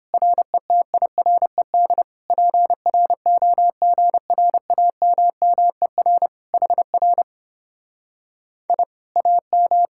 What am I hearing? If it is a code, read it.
Morse code, heard as a keyed tone, about 31 wpm, 700 Hz. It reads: RETIRED PROGRAMMER HR I AM